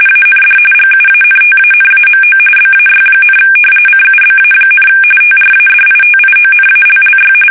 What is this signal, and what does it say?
RTTY, about 45.45 baud. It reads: AND GUANTANAMO BAY, CUBA, TRADITIONALLY
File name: fsk850 / battleshipnj